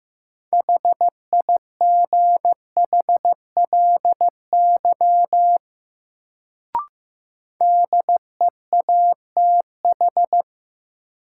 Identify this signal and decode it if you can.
Morse code, heard as a keyed tone, about 15 wpm, 700 Hz. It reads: HIGHLY DEATH
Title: Sentences from Top 1100 Words - New Word Drill - 15wpm